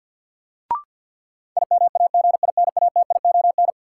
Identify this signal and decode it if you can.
Morse code, heard as a keyed tone, about 51 wpm, 700 Hz. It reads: IMAGINATION
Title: Sentences from Top 1300 Words - New Word Drill Alt - 50wpm